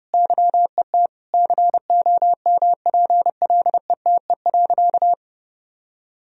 Morse code, heard as a keyed tone, about 32 wpm, 700 Hz. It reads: YET COMPLETE.
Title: Sentences from Top 1000 Words - Review Part 2 - 30wpm